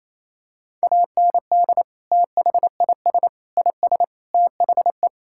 Morse code, heard as a keyed tone, about 28 wpm, 700 Hz. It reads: AND THIS IS THE
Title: Sentences from Top 1000 Words - Review Part 5 - 28wpm